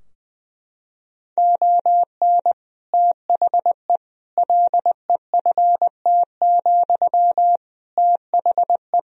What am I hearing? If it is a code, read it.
Morse code, heard as a keyed tone, about 20 wpm, 700 Hz. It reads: ON THE LEFT, THE
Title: Sentences from Top 1000 Words - Review Part 4 - 20wpm